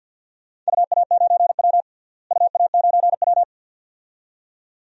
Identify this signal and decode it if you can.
Morse code, heard as a keyed tone, about 48 wpm, 700 Hz. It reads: UA9W UA9W
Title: DXCC-100 Call Signs - 4 Characters - Repeated 2x - 50wpm